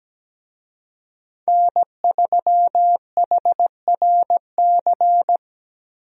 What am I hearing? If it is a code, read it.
Morse code, heard as a keyed tone, about 17 wpm, 700 Hz. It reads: N3HRC
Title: USA Call Signs - 5 Characters - Repeated 3x - 17wpm